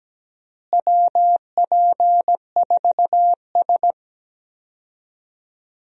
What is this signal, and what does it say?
Morse code, heard as a keyed tone, about 17 wpm, 700 Hz. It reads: WP4S
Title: USA Call Signs - 4 Characters - Repeated 3x - 17wpm